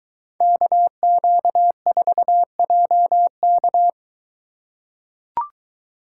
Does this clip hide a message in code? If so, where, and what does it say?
Yes: Morse code from 0.4 to 3.9 s, heard as a keyed tone, about 23 wpm, 700 Hz. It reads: KQ4JK